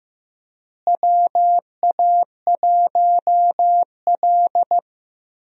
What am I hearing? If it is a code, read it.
Morse code, heard as a keyed tone, about 15 wpm, 700 Hz. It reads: WA1L
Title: USA Call Signs - 4 Characters - Repeated 3x - 15wpm